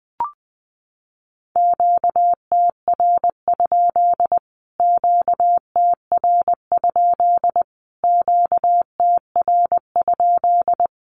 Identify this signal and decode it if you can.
Morse code, heard as a keyed tone, about 20 wpm, 700 Hz. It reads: QTR? QTR? QTR?
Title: Q-Codes Repeated 3x 20wpm